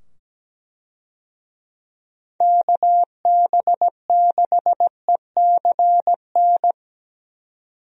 Morse code, heard as a keyed tone, about 17 wpm, 700 Hz. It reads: KB6ECN
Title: USA Call Signs - 6 Characters - 17wpm